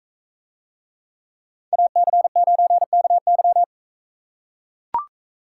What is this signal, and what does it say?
Morse code, heard as a keyed tone, about 44 wpm, 700 Hz. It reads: AC9KY